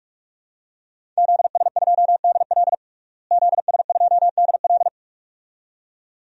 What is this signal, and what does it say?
Morse code, heard as a keyed tone, about 47 wpm, 700 Hz. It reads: ZS2DL ZS2DL
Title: DXCC-100 Call Signs - 5 Characters - Repeated 2x - 45wpm